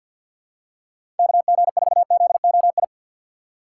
Morse code, heard as a keyed tone, about 48 wpm, 700 Hz. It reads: KG4ZOI
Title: USA Call Signs - 6 Characters - 4x Character Spacing - 50wpm